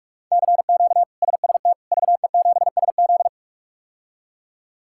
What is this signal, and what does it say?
Morse code, heard as a keyed tone, about 45 wpm, 700 Hz. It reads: CQ SST VE7SZ
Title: K1USN SST QSO Practice - 45wpm